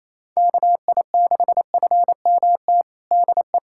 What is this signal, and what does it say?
Morse code, heard as a keyed tone, about 28 wpm, 700 Hz. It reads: KI6FMT DE
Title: USA Callsign DE Callsign - 6 Characters - 28wpm